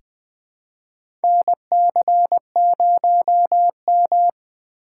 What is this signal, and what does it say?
Morse code, heard as a keyed tone, about 20 wpm, 700 Hz. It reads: NC0M
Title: USA Call Signs - 4 Characters - 2x Character Spacing - 20wpm